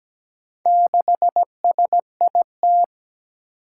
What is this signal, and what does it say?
Morse code, heard as a keyed tone, about 17 wpm, 700 Hz. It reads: 6SIT
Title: Four Challenging Letter-Number Combinations 17wpm